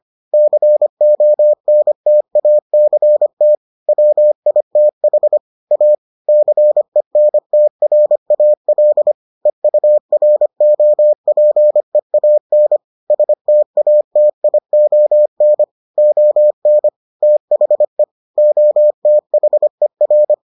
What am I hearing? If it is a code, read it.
Morse code, heard as a keyed tone, about 25 wpm, 600 Hz. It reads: CONTACT WITH A CENTRAL EUROPEAN STATION ON THE OTHER